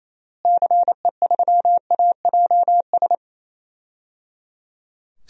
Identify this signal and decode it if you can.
Morse code, heard as a keyed tone, about 28 wpm, 700 Hz. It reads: CE3AJS
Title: DXCC-100 Call Signs - 6 Characters - Repeated 3x - 28wpm